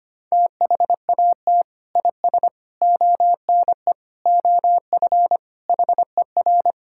Morse code, heard as a keyed tone, about 25 wpm, 700 Hz. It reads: THAT IS ONE OF HER
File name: practice / 25wpm / Sentences with 2-Word Phrases - 25wpm